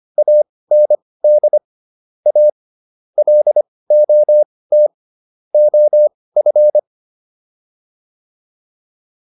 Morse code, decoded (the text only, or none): AND A LOT OF